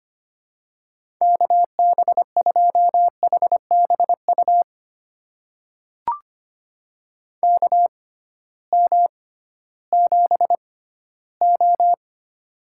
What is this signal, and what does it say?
Morse code, heard as a keyed tone, about 25 wpm, spaced as for 14 wpm, 700 Hz. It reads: KB2HBU KM7O
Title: USA Call Signs - 6x Character Spacing - 25wpm